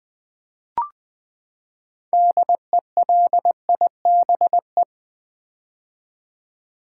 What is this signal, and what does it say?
Morse code, heard as a keyed tone, about 20 wpm, 700 Hz. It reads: DELIBE
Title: Build A Word - 6 Letters At A Time - 12-Letter Words - 20wpm